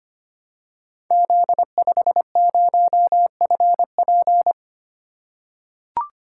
Morse code, heard as a keyed tone, about 25 wpm, 690 Hz. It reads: Z50FP
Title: Code Groups - 5 Letter:Numbers - 25wpm